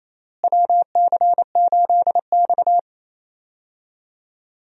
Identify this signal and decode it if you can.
Morse code, heard as a keyed tone, about 28 wpm, 700 Hz. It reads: WC8X
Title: USA Call Signs - 4 Characters - Repeated 3x - 28wpm